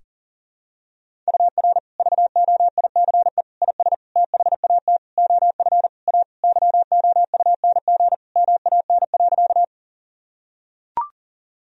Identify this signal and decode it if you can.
Morse code, heard as a keyed tone, about 38 wpm, 700 Hz. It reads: UR VOICE IS THAT OF A YOUNG MAN.
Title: Sentences from Top 500 Words and Abbreviations 40wpm